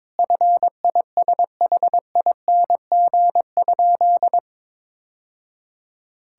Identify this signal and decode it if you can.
Morse code, heard as a keyed tone, about 21 wpm, 700 Hz. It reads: FISHING?
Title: Sentences from Top 1000 Words - Review Part 5 - 22wpm